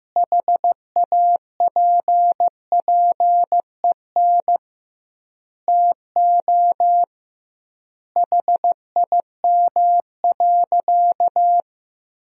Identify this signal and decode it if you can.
Morse code, heard as a keyed tone, about 15 wpm, 700 Hz. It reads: HAPPEN TO HIM.
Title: 2x Word-Spacing - Sentences from Top 400 Words - 15wpm